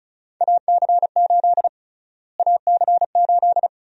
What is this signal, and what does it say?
Morse code, heard as a keyed tone, about 36 wpm, 700 Hz. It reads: AC8 AC8
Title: USA Call Sign Prefixes - Repeated 2x - 35wpm